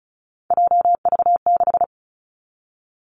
Morse code, heard as a keyed tone, about 36 wpm, 700 Hz. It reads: JV6
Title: Three Challenging Letter-Number Combinations 35wpm